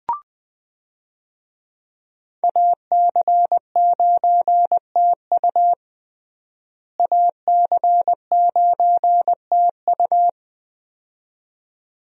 Morse code, heard as a keyed tone, about 20 wpm, 700 Hz. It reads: AC9TU AC9TU
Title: USA Call Signs - 5 Characters - Repeated 2x - 20wpm